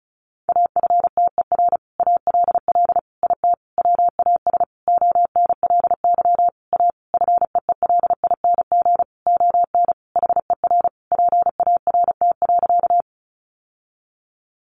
Morse code, heard as a keyed tone, about 34 wpm, 700 Hz. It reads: AFTER ALL IT WAS ONLY A FEELING ON HER PART.
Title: Sentences from Top 1200 Words 35wpm